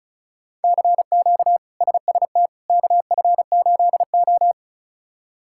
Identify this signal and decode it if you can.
Morse code, heard as a keyed tone, about 36 wpm, 700 Hz. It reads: CQ SST KF8O
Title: K1USN SST QSO Practice - 35wpm